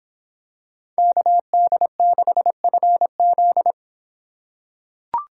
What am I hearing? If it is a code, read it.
Morse code, heard as a keyed tone, about 26 wpm, 700 Hz. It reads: KD6FZ